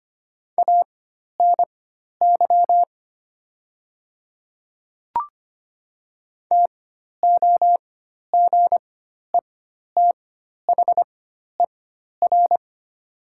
Morse code, heard as a keyed tone, about 25 wpm, spaced as for 12 wpm, 700 Hz. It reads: ANY TOGETHER
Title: 4x Character Spacing - Top 300 Words - 25wpm